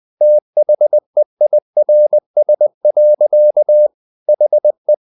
Morse code, heard as a keyed tone, about 20 wpm, 600 Hz. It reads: THEIRS. HE
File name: the old man and the sea / the old man and the sea part 1 20wpm